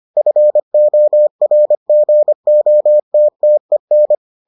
Morse code, heard as a keyed tone, about 25 wpm, 600 Hz. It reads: FORGOTTEN